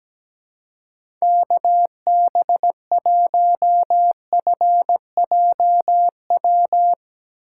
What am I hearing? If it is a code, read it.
Morse code, heard as a keyed tone, about 17 wpm, 700 Hz. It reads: KB1FJW
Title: USA Call Signs - 6 Characters - 3x Character Spacing - 17wpm